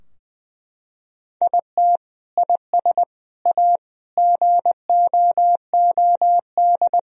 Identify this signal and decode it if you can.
Morse code, heard as a keyed tone, about 21 wpm, 700 Hz. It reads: IT IS A GOOD
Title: Sentences from Top 1000 Words - Review Part 4 - 20wpm